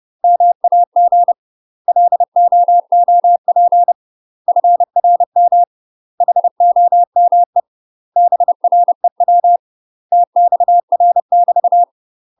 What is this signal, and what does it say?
Morse code, heard as a keyed tone, about 30 wpm, 700 Hz. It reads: MAG LOOP FRM HOME BREW TXR <BT>
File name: UK MASTER QSOs TRAINING / 30WPM / UK MASTER QSOs 0039